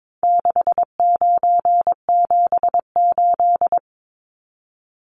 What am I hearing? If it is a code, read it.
Morse code, heard as a keyed tone, about 22 wpm, 700 Hz. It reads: 6978
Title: Numbers - 4-Digit 22wpm